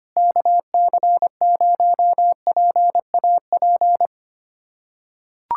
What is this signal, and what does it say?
Morse code, heard as a keyed tone, about 25 wpm, 700 Hz. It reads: KC0PAP
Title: USA Callsign DE Callsign 25wpm